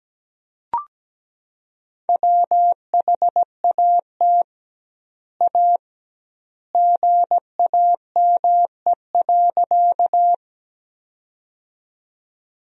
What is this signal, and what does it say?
Morse code, heard as a keyed tone, about 17 wpm, 700 Hz. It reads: WHAT A GAME.
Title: 2x Word-Spacing - Sentences from Top 500 Words - 17wpm